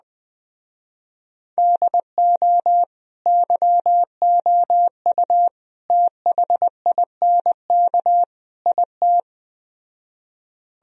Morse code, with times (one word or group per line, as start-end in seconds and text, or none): 1.6-2.8 DO
3.3-5.5 YOU
5.9-8.2 THINK
8.7-9.2 IT